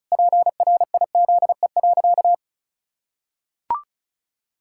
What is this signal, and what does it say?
Morse code, heard as a keyed tone, about 36 wpm, 700 Hz. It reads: PRIZE.
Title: Pangram Challenge v2 - 35wpm